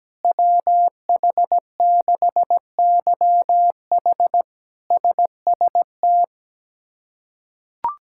Morse code, heard as a keyed tone, about 17 wpm, 700 Hz. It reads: WH6YH SST
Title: K1USN SST QSO Practice - 17wpm